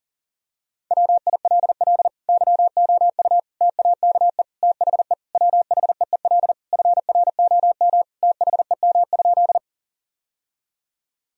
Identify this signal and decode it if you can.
Morse code, heard as a keyed tone, about 40 wpm, 700 Hz. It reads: WILL YOU TAKE THE WHEEL FROM THEM?
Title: Sentences from Top 1000 Words - Review Part 3 - 40wpm